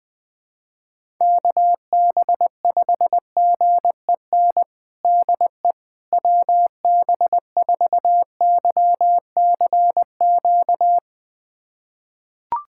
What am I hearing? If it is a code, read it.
Morse code, heard as a keyed tone, about 20 wpm, 700 Hz. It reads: KB5GEN DE WB4YCQ
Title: USA Callsign DE Callsign - 6 Characters - 20wpm